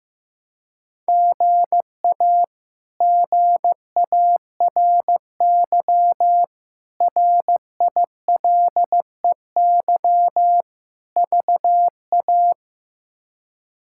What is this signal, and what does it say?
Morse code, heard as a keyed tone, about 15 wpm, 700 Hz. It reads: GA GARY RILEY VA
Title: K1USN SST QSO Practice - 15wpm